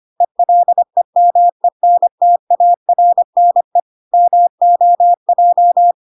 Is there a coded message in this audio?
Yes: Morse code from 0.2 to 5.9 s, heard as a keyed tone, about 25 wpm, 700 Hz. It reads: ELEMENTARNE MOJ